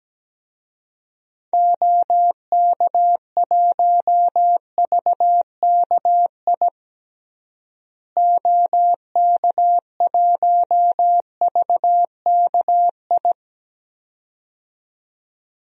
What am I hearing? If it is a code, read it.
Morse code, heard as a keyed tone, about 17 wpm, 700 Hz. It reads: OK1VKI OK1VKI
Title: DXCC-100 Call Signs - 6 Characters - Repeated 2x - 17wpm